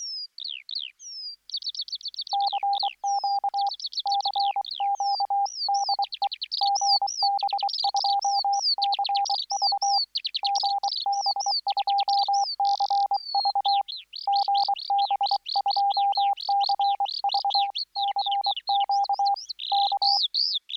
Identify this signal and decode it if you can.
Morse code, heard as a keyed tone, about 24 wpm, 800 Hz. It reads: CQ CQ DE GB2CV GB2CV GB2CV CQ K